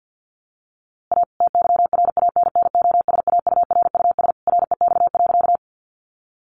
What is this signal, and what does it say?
Morse code, heard as a keyed tone, about 50 wpm, 700 Hz. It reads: A TYRANNOSAURUS REX.
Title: A Sound Of Thunder 50wpm